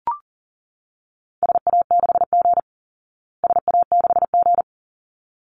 Morse code, heard as a keyed tone, about 40 wpm, 700 Hz. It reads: SA6G SA6G
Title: Call Signs Repeated 3x - 40wpm